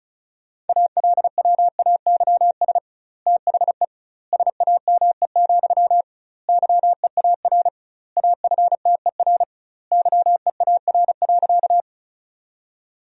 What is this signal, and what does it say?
Morse code, heard as a keyed tone, about 34 wpm, 700 Hz. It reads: ALWAYS THE SAME, YEAR AFTER YEAR.
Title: 2x Word-Spacing - Sentences from Top 300 Words - 35wpm